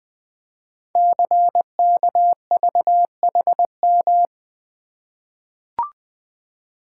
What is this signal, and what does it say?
Morse code, heard as a keyed tone, about 20 wpm, 700 Hz. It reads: CKVHM
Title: Code Groups - 5 Letters - 20wpm